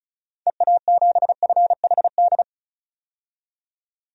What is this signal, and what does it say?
Morse code, heard as a keyed tone, about 36 wpm, 700 Hz. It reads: EA7FHD
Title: DXCC-100 Call Signs - 6 Characters - 35wpm